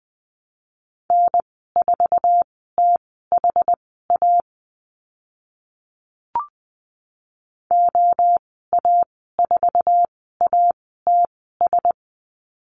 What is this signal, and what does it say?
Morse code, heard as a keyed tone, about 20 wpm, spaced as for 15 wpm, 700 Hz. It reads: N4THA OA4ATS
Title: 076 Lesson 13g - Letter L - Call Signs 20wpm